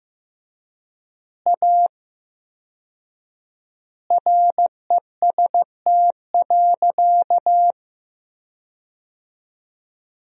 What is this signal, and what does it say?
Morse code, heard as a keyed tone, about 15 wpm, 700 Hz. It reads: A REST.
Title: 4x Word-Spacing - Sentences from Top 500 Words - 15wpm